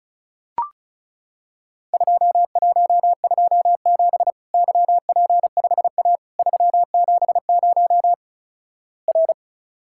Morse code, heard as a keyed tone, about 36 wpm, 700 Hz. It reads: 2127 YP5A 370 R
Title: Worked All Europe DX Contest - QTC Exchange - 35wpm